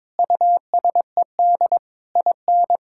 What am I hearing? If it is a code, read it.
Morse code, heard as a keyed tone, about 22 wpm, 700 Hz. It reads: USED IN
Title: Sentences from Top 1000 Words - Review Part 6 - 22wpm